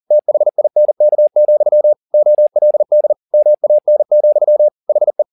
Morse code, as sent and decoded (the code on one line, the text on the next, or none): - .... .. -. -.- --..-- / --- .-.. -.. / -- .- -. --..-- / .... .
THINK, OLD MAN, HE